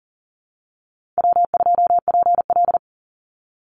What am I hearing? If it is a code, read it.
Morse code, heard as a keyed tone, about 38 wpm, 700 Hz. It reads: W2PL
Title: Easy Call Signs Repeated 3x - 40wpm